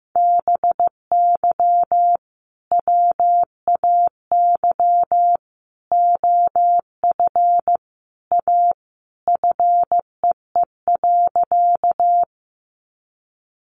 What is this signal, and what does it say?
Morse code, heard as a keyed tone, about 15 wpm, 700 Hz. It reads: BY WAY OF A FEE.
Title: Sentences from Top 1500 Words 15wpm